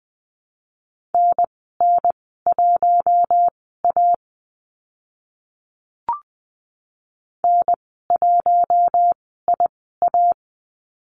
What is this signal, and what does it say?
Morse code, heard as a keyed tone, about 20 wpm, spaced as for 15 wpm, 700 Hz. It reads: NN1A N1IA
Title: 042 Lesson 8g - Number 1 - Call Signs 20wpm